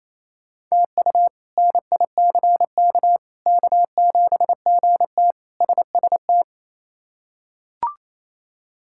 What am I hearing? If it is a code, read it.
Morse code, heard as a keyed tone, about 27 wpm, 700 Hz. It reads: TU NICK K7GT SST
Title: K1USN SST QSO Practice - 28wpm